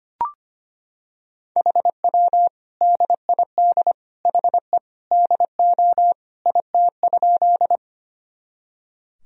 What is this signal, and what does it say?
Morse code, heard as a keyed tone, about 25 wpm, 700 Hz. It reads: HW DID HE DO IT?
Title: Sentences from Top 100 Words and Abbreviations 25wpm